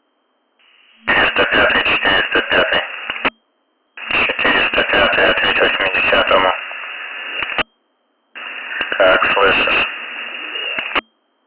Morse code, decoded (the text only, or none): none